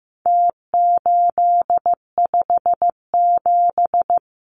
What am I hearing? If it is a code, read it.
Morse code, heard as a keyed tone, about 15 wpm, 700 Hz. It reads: T857
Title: Numbers - Contest Style - 4-Digit 15wpm